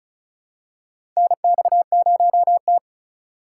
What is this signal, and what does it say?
Morse code, heard as a keyed tone, about 34 wpm, 700 Hz. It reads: NX0T
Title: USA Call Signs - 4 Characters - 2x Character Spacing - 35wpm